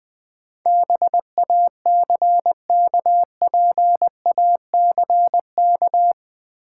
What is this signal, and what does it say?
Morse code, heard as a keyed tone, about 21 wpm, 700 Hz. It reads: BACKPACK
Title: Easy Words with Hints - 20wpm